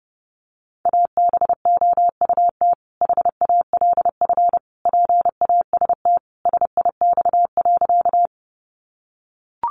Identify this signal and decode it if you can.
Morse code, heard as a keyed tone, about 30 wpm, 700 Hz. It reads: ABOUT HALF PAST SIX.
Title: Sentences from Top 600 Words 30wpm